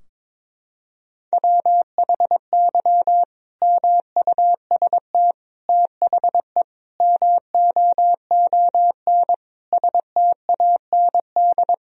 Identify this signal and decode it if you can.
Morse code, heard as a keyed tone, about 22 wpm, 700 Hz. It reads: WHY MUST THE MOON STAND
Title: Sentences from Top 1000 Words - Review Part 3 - 22wpm